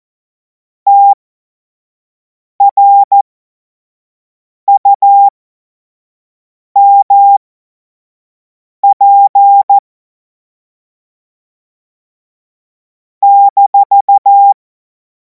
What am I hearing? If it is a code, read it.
Morse code, heard as a keyed tone, about 14 wpm, spaced as for 5 wpm, 800 Hz. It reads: TRUMP -
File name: Morsekurs / Witze / Witz 09. Juni 2026 25 F70